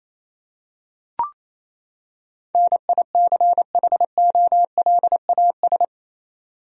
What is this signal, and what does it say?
Morse code, heard as a keyed tone, about 28 wpm, 700 Hz. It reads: NICHOLAS
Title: USA Common Names - 28wpm